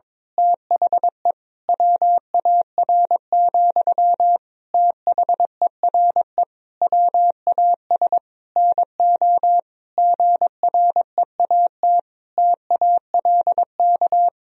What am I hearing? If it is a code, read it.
Morse code, heard as a keyed tone, about 22 wpm, 700 Hz. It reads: THE WAR, THERE WAS NO GREAT TALK
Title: Sentences from Top 1000 Words - Review Part 2 - 22wpm